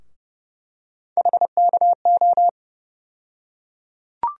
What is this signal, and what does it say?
Morse code, heard as a keyed tone, about 31 wpm, 700 Hz. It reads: HKO
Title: USA Call Sign Postfixes - 3 Letters 30wpm